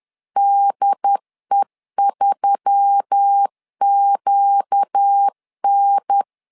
Morse code, heard as a keyed tone, about 10 wpm, 800 Hz. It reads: DE3QN
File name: LEZIONI / 4 Q1S3 10wpm 2427